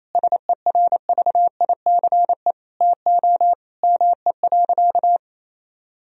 Morse code, heard as a keyed tone, about 28 wpm, 700 Hz. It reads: SERVICE TO ME.
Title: Sentences from Top 1000 Words - Review Part 4 - 28wpm